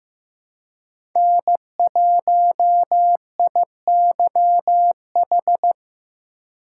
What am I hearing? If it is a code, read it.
Morse code, heard as a keyed tone, about 15 wpm, 700 Hz. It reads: N1IYH
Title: USA Call Signs - 2x Character Spacing - 15wpm